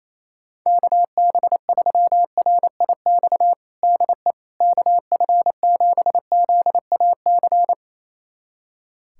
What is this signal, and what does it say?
Morse code, heard as a keyed tone, about 28 wpm, 700 Hz. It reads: KB3RIX DE KF7ZAC
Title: USA Callsign DE Callsign - 6 Characters - 28wpm